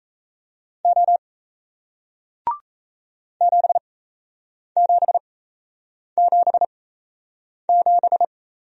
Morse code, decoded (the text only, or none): O 7777